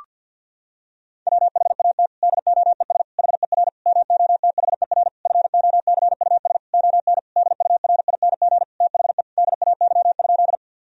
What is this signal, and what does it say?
Morse code, heard as a keyed tone, about 50 wpm, 700 Hz. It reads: WHAT DOES HER MOTHER FOCUS ON DURING THE DAY?
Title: Sentences from Top 1000 Words - Review Part 4 - 50wpm